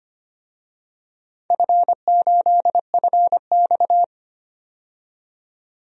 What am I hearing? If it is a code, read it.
Morse code, heard as a keyed tone, about 25 wpm, 700 Hz. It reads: F8FX
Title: DXCC-100 Call Signs - 4 Characters - 25wpm